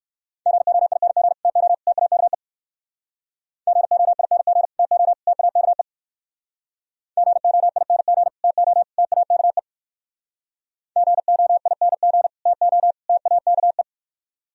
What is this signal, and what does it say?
Morse code, heard as a keyed tone, about 53 wpm, 700 Hz. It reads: GOING TO TAKE GOING TO TAKE GOING TO TAKE GOING TO TAKE
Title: Speed-Racing - Sets of 3 Words 45wpm